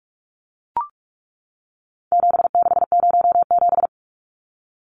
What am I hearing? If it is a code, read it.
Morse code, heard as a keyed tone, about 44 wpm, 700 Hz. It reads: 7607
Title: Numbers - 4-Digit 45wpm